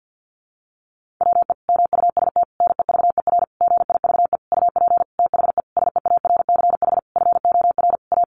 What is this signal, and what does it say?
Morse code, heard as a keyed tone, about 50 wpm, 700 Hz. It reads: WE MUST NEVER GIVE UP THE SEARCH FOR A
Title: Sentences from Top 1500 Words 50wpm